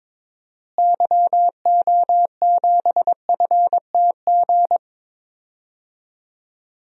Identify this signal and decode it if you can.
Morse code, heard as a keyed tone, about 21 wpm, 700 Hz. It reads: YO7FTG
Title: DXCC-100 Call Signs - 6 Characters - Repeated 3x - 22wpm